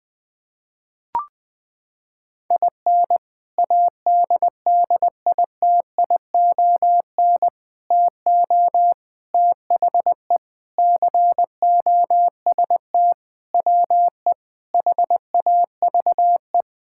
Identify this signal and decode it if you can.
Morse code, heard as a keyed tone, about 20 wpm, 700 Hz. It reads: IN ADDITION TO THE COST WE HAVE
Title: Sentences with 4-Word Phrases - 20wpm